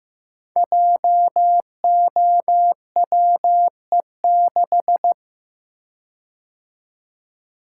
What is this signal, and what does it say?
Morse code, heard as a keyed tone, about 15 wpm, 700 Hz. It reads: JOWE6
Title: Code Groups - 5 Letter:Numbers - 15wpm